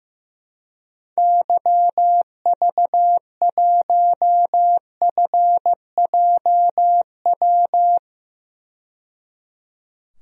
Morse code, heard as a keyed tone, about 15 wpm, 700 Hz. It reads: YV1FJW
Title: DXCC-100 Call Signs - 6 Characters - 15wpm